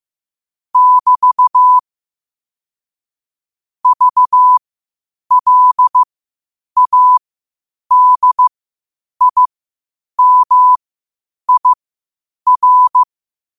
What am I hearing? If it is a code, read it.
Morse code, heard as a keyed tone, about 15 wpm, spaced as for 8 wpm, 1000 Hz. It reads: <BT> VLADIMIR
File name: qotdfiles / 8wpm